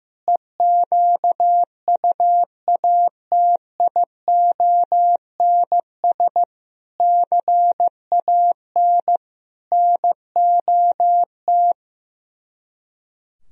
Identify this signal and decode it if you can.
Morse code, heard as a keyed tone, about 15 wpm, 700 Hz. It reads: EQUATIONS CAN NOT
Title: Speed-Racing - Sets of 5 Words 15wpm